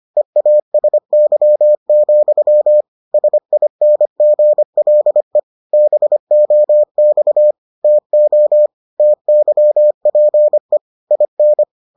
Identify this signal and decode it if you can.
Morse code, heard as a keyed tone, about 25 wpm, 600 Hz. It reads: EASY, SINGLE BOX TO TYPE IN